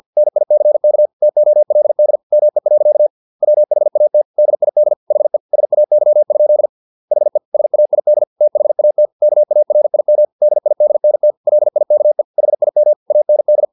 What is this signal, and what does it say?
Morse code, heard as a keyed tone, about 50 wpm, 600 Hz. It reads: NICK TOLD ME. WHAT DID HE SAY? HE SAID THAT KARIM DIDNT LIKE HIM AND
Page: Never 49wpm